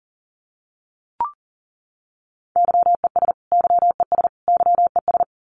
Morse code, heard as a keyed tone, about 40 wpm, 700 Hz. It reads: YES YES YES
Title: Top 500 Words Repeated 40wpm